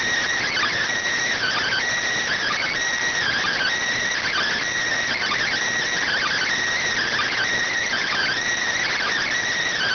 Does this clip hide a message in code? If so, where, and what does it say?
Yes: Emergency Alert System (SAME) tones (header bursts) from 0.9 to 9.0 s, heard as harsh buzzing bursts.